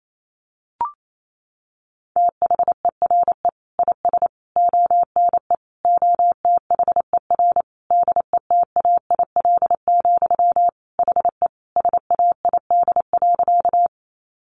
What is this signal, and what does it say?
Morse code, heard as a keyed tone, about 28 wpm, 700 Hz. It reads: THERE IS ONE OTHER DETAIL, HE SAID.
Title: Sentences from Top 1500 Words 28wpm